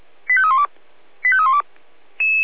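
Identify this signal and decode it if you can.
Five-tone selective call (ZVEI), heard as quick runs of five tones, tones of about 70 ms. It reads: 85312 85312
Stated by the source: ZVEI five-tone call 85320 received on an SDR.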